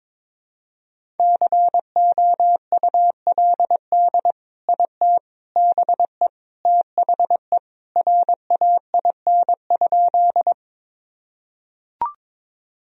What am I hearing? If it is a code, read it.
Morse code, heard as a keyed tone, about 22 wpm, 700 Hz. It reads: COULD IT BE THE RAIN?
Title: Sentences from Top 1000 Words - Review Part 1 - 22wpm